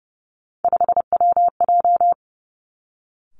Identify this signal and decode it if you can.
Morse code, heard as a keyed tone, about 30 wpm, 700 Hz. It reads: 5WJ
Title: Three Challenging Letter-Number Combinations 30wpm